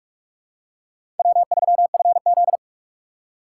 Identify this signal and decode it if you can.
Morse code, heard as a keyed tone, about 46 wpm, 700 Hz. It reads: W3FZ
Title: USA Call Signs - 4 Characters - 2x Character Spacing - 45wpm